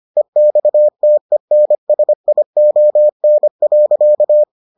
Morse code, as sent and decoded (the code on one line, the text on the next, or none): . -..- - . -. ... .. --- -. .-.-.-
EXTENSION.